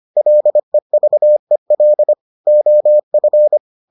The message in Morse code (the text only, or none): LEVEL OF